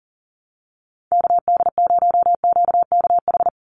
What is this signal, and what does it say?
Morse code, heard as a keyed tone, about 40 wpm, 700 Hz. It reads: KD0QKH
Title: USA Call Signs 40wpm